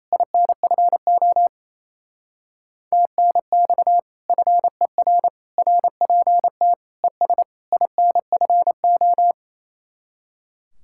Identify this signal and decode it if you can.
Morse code, heard as a keyed tone, about 35 wpm, 700 Hz. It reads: INFO TNX FER RPT ES INFO
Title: Speed-Racing Common QSO Phrases - 28wpm